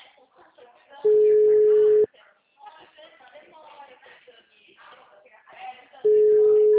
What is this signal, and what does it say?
Ringback tone (Europe, 425 Hz), with speech.